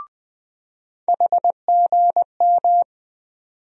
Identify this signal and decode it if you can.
Morse code, heard as a keyed tone, about 20 wpm, 700 Hz. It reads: HGM
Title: USA Call Sign Postfixes - 3 Letters 20wpm